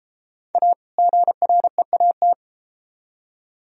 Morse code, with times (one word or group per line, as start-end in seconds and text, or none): 0.6-0.7 A
1.0-2.3 GREAT